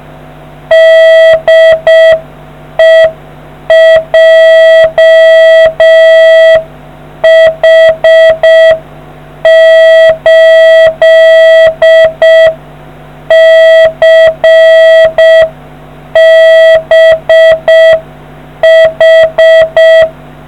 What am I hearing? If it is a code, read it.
Morse code, heard as a keyed tone, about 6 wpm, 1950 Hz. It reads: DE JH8CBH